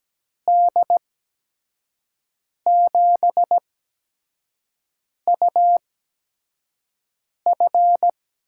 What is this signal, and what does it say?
Morse code, heard as a keyed tone, about 17 wpm, spaced as for 5 wpm, 700 Hz. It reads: D7UF
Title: USA Call Signs - 8x Character Spacing - 17wpm